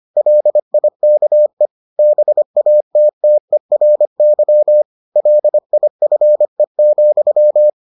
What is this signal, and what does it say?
Morse code, heard as a keyed tone, about 25 wpm, 600 Hz. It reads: LIKE BATTERY LIFE,